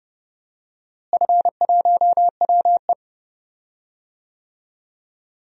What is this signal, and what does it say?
Morse code, heard as a keyed tone, about 31 wpm, 700 Hz. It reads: F1WE
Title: DXCC-100 Call Signs - 4 Characters - Repeated 2x - 30wpm